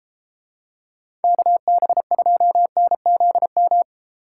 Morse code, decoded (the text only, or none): KB2NZM